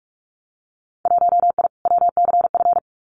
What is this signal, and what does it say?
Morse code, heard as a keyed tone, about 44 wpm, 690 Hz. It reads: 1I WCF
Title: Winter Field Day Exchanges Encore 45wpm